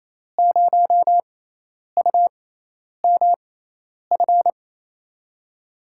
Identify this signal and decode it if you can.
Morse code, heard as a keyed tone, about 28 wpm, spaced as for 10 wpm, 700 Hz. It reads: 0UMF
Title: USA Call Signs - 6 Characters - 6x Character Spacing - 28wpm